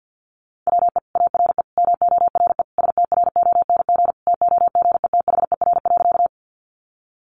Morse code, heard as a keyed tone, about 50 wpm, 700 Hz. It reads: WE ARE MORE STRONG TOGETHER.
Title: Sentences from Top 500 Words 50wpm